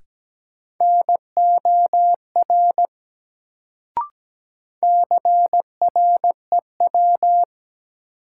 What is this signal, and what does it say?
Morse code, heard as a keyed tone, about 17 wpm, 700 Hz. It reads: NOR CREW
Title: Sentences from Top 1500 Words - New Word Drill - 17wpm